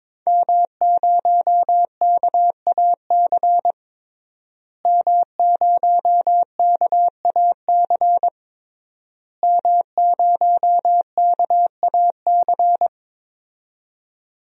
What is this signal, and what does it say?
Morse code, heard as a keyed tone, about 22 wpm, 700 Hz. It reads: M0KAC M0KAC M0KAC
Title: DXCC-100 Call Signs - 5 Characters - Repeated 3x - 22wpm